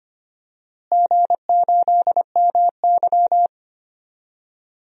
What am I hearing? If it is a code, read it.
Morse code, heard as a keyed tone, about 25 wpm, 700 Hz. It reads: G8MY
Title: DXCC-100 Call Signs - 4 Characters - Repeated 3x - 25wpm